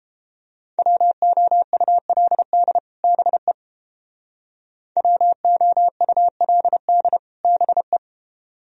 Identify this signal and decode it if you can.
Morse code, heard as a keyed tone, about 31 wpm, 700 Hz. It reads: WOULD BE WOULD BE